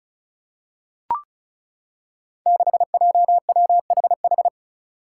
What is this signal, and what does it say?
Morse code, heard as a keyed tone, about 36 wpm, 700 Hz. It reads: 6JWHH
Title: Five Challenging Letter-Number Combinations 35wpm